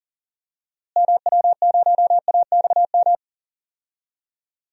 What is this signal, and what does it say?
Morse code, heard as a keyed tone, about 40 wpm, 700 Hz. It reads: MW0AXM
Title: DXCC-100 Call Signs - 6 Characters - Repeated 2x - 40wpm